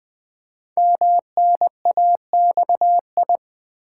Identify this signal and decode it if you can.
Morse code, heard as a keyed tone, about 20 wpm, 700 Hz. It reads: MNAXI
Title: Code Groups - 5 Letters - 20wpm